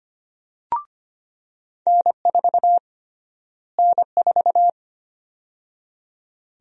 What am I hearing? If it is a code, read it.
Morse code, heard as a keyed tone, about 25 wpm, 700 Hz. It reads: N4 N4
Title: USA Call Sign Prefixes - Repeated 2x - 25wpm